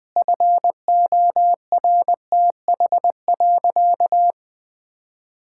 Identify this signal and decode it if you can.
Morse code, heard as a keyed tone, about 20 wpm, 700 Hz. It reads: FORTH.
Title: Sentences with 6-Word Phrases - Spoken Beginning - 20wpm